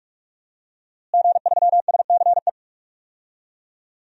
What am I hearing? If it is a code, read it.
Morse code, heard as a keyed tone, about 46 wpm, 700 Hz. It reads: G3SCE
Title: DXCC-100 Call Signs - 5 Characters - Repeated 2x - 45wpm